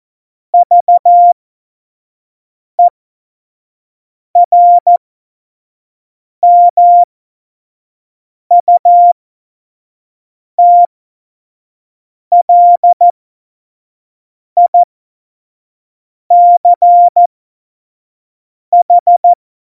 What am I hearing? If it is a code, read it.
Morse code, heard as a keyed tone, about 14 wpm, spaced as for 5 wpm, 700 Hz. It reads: VERMUTLICH
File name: Morsekurs / Witze / Witz 09. April 2019, 25 F70